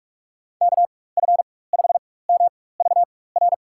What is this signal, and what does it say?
Morse code, heard as a keyed tone, about 47 wpm, spaced as for 21 wpm, 700 Hz. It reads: KF5MVR
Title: USA Call Signs - 6 Characters - 4x Character Spacing - 45wpm